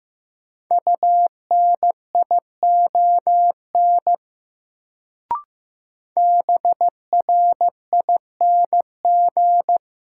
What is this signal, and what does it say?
Morse code, heard as a keyed tone, about 16 wpm, 700 Hz. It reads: UNION BRING
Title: Sets of 5 Words Encore - New Word Drill - 15wpm